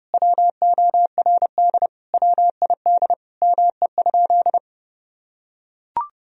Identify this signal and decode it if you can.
Morse code, heard as a keyed tone, about 30 wpm, 700 Hz. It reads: WORD WID ME?
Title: Sentences from Top 300 Words and Abbreviations 30wpm